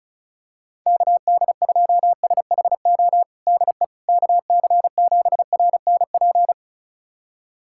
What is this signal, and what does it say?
Morse code, heard as a keyed tone, about 36 wpm, 700 Hz. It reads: KD2SHO DE KC7RNP
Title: USA Callsign DE Callsign - 6 Characters - 35wpm